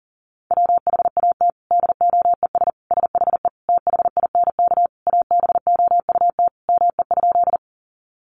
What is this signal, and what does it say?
Morse code, heard as a keyed tone, about 40 wpm, 700 Hz. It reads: WHAT DOES SHE THINK ABOUT ME?
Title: Sentences from Top 200 Words 40wpm